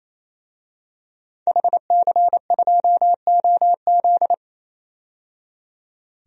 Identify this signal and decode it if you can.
Morse code, heard as a keyed tone, about 28 wpm, 700 Hz. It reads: HC2OZ
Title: DXCC-100 Call Signs - 5 Characters - 28wpm